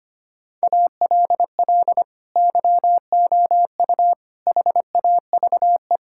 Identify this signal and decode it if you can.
Morse code, heard as a keyed tone, about 25 wpm, 700 Hz. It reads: ALL YOU HAVE
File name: practice / 25wpm / Sentences with 3-Word Phrases - 25wpm